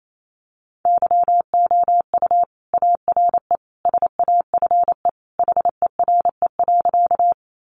Morse code, heard as a keyed tone, about 28 wpm, 700 Hz. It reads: YOU ARE SAFE HERE.
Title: Sentences from Top 900 Words 28wpm